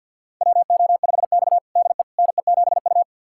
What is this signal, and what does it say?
Morse code, heard as a keyed tone, about 47 wpm, 700 Hz. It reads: WO5X DE NE7U
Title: USA Callsign DE Callsign - 4 Characters - 50wpm